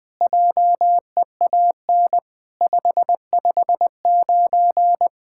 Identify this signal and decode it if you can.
Morse code, heard as a keyed tone, about 19 wpm, 700 Hz. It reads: JEAN 559
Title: SOTA QSO Practice 20wpm